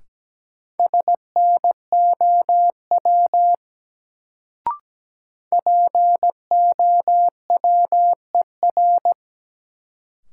Morse code, heard as a keyed tone, about 17 wpm, 700 Hz. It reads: SNOW POWER
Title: Sentences from Top 500 Words - New Word Drill - 17wpm